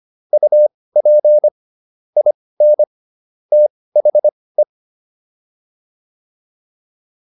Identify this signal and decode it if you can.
Morse code, heard as a keyed tone, about 25 wpm, spaced as for 18 wpm, 600 Hz. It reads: UP IN THE